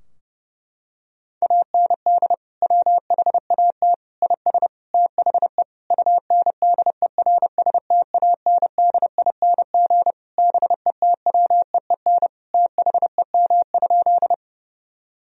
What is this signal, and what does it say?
Morse code, heard as a keyed tone, about 32 wpm, 700 Hz. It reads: AND WHAT IS THE UNDERSTANDING BETWEEN THEM?
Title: Sentences from Top 1000 Words - Review Part 3 - 30wpm